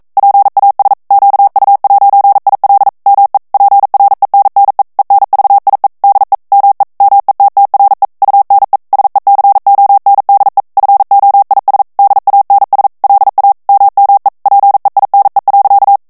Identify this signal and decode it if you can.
Morse code, heard as a keyed tone, about 42 wpm, 800 Hz. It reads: PAS QU'IL ME PRENNE ENVIE DE ME METTRE UNE SECONDE FOIS DANS LA MME PEINE.
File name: CW / FICHIERS-MP3 / FRANCAIS TEXTE CLAIR 2 VITESSE 40 MOTS MINUTES